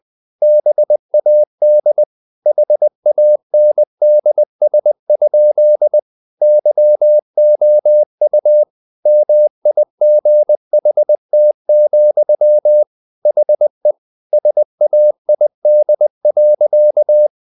Morse code, heard as a keyed tone, about 20 wpm, 600 Hz. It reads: BAD HANDS? YOU MIGHT, HE SAID.
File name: the old man and the sea part 2 20wpm